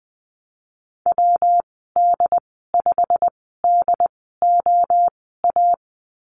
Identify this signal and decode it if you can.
Morse code, heard as a keyed tone, about 20 wpm, spaced as for 14 wpm, 700 Hz. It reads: WD5DOA
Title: 154 Lesson 25g - Letter p - Call Signs 20wpm